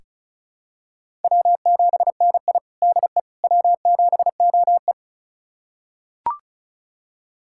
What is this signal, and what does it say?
Morse code, heard as a keyed tone, about 36 wpm, 700 Hz. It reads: W7NI DE W7OE
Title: USA Callsign DE Callsign - 4 Characters - 35wpm